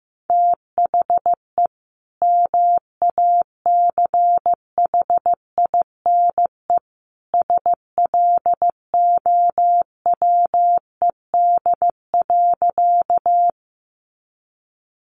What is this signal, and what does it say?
Morse code, heard as a keyed tone, about 15 wpm, 700 Hz. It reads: THE MACHINE SLOWED.
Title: A Sound Of Thunder 15wpm